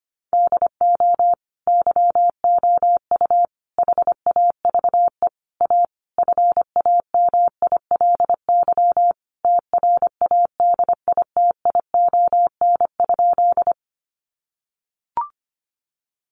Morse code, heard as a keyed tone, about 25 wpm, 700 Hz. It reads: DO YOU HAVE A FAMILY TRADITION?
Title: Sentences from Top 1500 Words 25wpm